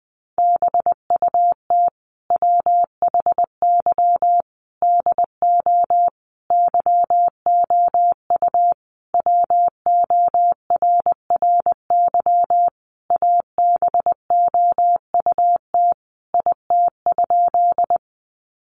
Morse code, heard as a keyed tone, about 20 wpm, 700 Hz. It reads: BUT WHY DO YOU WORRY ABOUT IT?
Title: Sentences from Top 1100 Words 20wpm